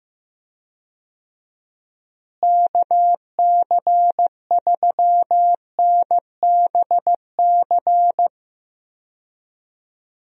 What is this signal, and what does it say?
Morse code, heard as a keyed tone, about 15 wpm, 700 Hz. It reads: KC3NBC
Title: USA Call Signs - 6 Characters - Repeated 2x - 15wpm